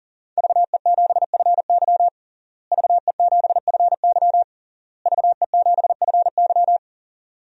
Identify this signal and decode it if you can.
Morse code, heard as a keyed tone, about 40 wpm, 700 Hz. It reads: VE7FY VE7FY VE7FY
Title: DXCC-100 Call Signs - 5 Characters - Repeated 3x - 40wpm